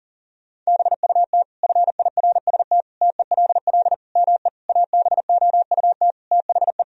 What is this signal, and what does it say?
Morse code, heard as a keyed tone, about 37 wpm, 700 Hz. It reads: BUT FIRST TELL ME ABOUT THE
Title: Sentences from Top 1000 Words - Review Part 2 - 40wpm